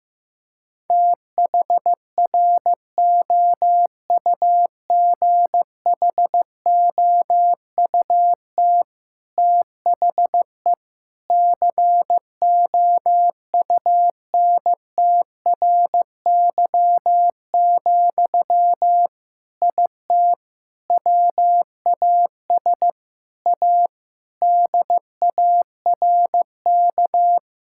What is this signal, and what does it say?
Morse code, heard as a keyed tone, about 16 wpm, 700 Hz. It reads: THROUGHOUT THE COUNTRY, IT WAS A DARK
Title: Sentences from Top 1000 Words - Review Part 5 - 15wpm